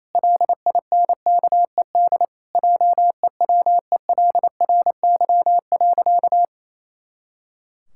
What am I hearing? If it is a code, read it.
Morse code, heard as a keyed tone, about 28 wpm, 700 Hz. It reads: LINKED JEWELRY.
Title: Pangram Challenge v2 - 28wpm